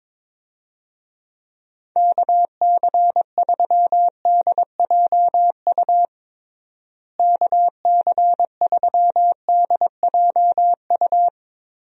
Morse code, heard as a keyed tone, about 22 wpm, 700 Hz. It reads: KC3DJU KC3DJU
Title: USA Call Signs - 6 Characters - Repeated 3x - 22wpm